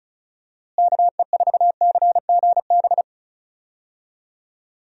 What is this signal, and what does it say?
Morse code, heard as a keyed tone, about 34 wpm, 700 Hz. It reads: KE4CGB
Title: USA Call Signs - 6 Characters - Repeated 3x - 35wpm